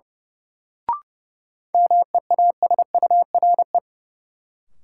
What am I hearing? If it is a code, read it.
Morse code, heard as a keyed tone, about 30 wpm, 700 Hz. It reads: MEASURE
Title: Sentences from Top 1000 Words - Review Part 1 - New Word Drill - 30wpm